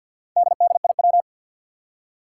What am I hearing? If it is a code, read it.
Morse code, heard as a keyed tone, about 47 wpm, 700 Hz. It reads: DDIW
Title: Four Challenging Letter-Number Combinations 50wpm